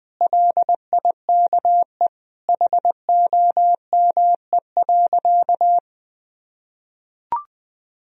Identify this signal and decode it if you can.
Morse code, heard as a keyed tone, about 21 wpm, 700 Hz. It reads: LIKE HOME.
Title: Sentences from Top 200 Words and Abbreviations 20wpm